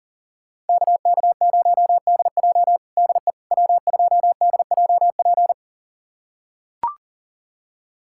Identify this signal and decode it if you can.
Morse code, heard as a keyed tone, about 40 wpm, 700 Hz. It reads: KK0DJ DE W2DJP
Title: USA Callsign DE Callsign - 5 Characters - 40wpm